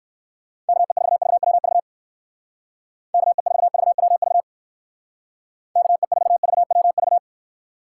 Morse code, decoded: KE4VWV KE4VWV KE4VWV

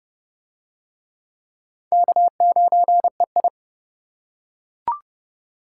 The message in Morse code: K9EI

